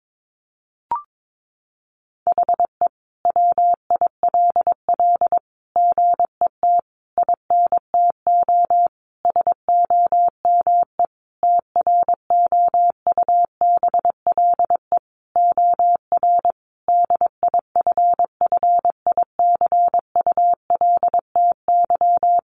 Morse code, heard as a keyed tone, about 22 wpm, 700 Hz. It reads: HE WILL GET INTO SOME TROUBLE OR DIFFICULTY
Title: Sentences from Top 1500 Words 22wpm